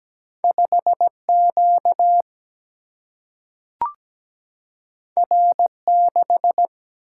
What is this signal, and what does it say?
Morse code, heard as a keyed tone, about 17 wpm, 700 Hz. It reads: 5 Q R 6
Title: Single-Letter Single-Number 17wpm